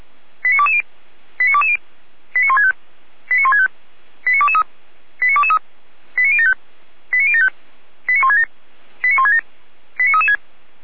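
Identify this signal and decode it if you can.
Five-tone selective call (ZVEI), heard as quick runs of five tones, tones of about 70 ms. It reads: 89220 89220 89165 89165 89223 89223 89075 89075 89167 89167 89336
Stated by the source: ZVEI five-tone selective call received on an SDR.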